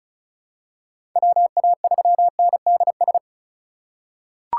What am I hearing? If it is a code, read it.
Morse code, heard as a keyed tone, about 36 wpm, 700 Hz. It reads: WA3NDS
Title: USA Call Signs - 6 Characters - 3x Character Spacing - 35wpm